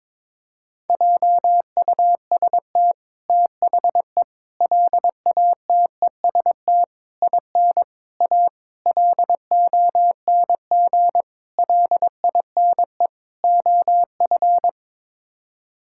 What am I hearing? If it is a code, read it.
Morse code, heard as a keyed tone, about 22 wpm, 700 Hz. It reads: JUST THE LATEST IN A LONG LINE OF
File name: practice / 22wpm / Sets of 8 Words - 22wpm